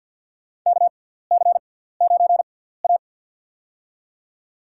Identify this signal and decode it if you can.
Morse code, heard as a keyed tone, about 46 wpm, spaced as for 17 wpm, 700 Hz. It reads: KC9A
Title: USA Call Signs - 6x Character Spacing - 50wpm